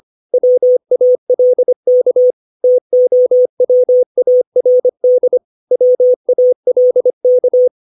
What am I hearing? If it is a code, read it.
Morse code, heard as a keyed tone, about 25 wpm, 500 Hz. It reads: WALK TOWARD WALK
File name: 632 words / 632 Words 25WPM